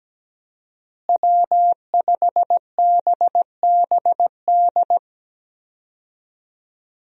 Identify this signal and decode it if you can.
Morse code, heard as a keyed tone, about 17 wpm, 700 Hz. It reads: W5BBD
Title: USA Call Signs - 5 Characters - Repeated 2x - 17wpm